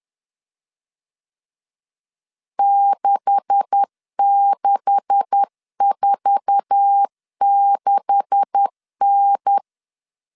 Morse code, heard as a keyed tone, about 10 wpm, 800 Hz. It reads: 6646N
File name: LEZIONI / 9 286 10wpm 2735